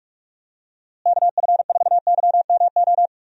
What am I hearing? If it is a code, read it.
Morse code, heard as a keyed tone, about 46 wpm, 700 Hz. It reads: KF4YMO
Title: USA Call Signs - 6 Characters - Repeated 3x - 45wpm